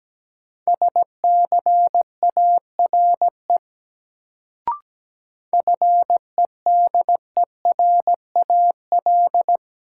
Morse code, heard as a keyed tone, about 17 wpm, 700 Hz. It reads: SCARE FEDERAL
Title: Sentences from Top 1000 Words - Review Part 5 - New Word Drill - 17wpm